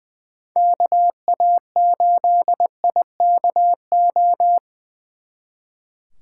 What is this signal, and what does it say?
Morse code, heard as a keyed tone, about 24 wpm, 700 Hz. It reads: KA8IKO
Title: Speed Racing - USA 2x3 Call Signs 20wpm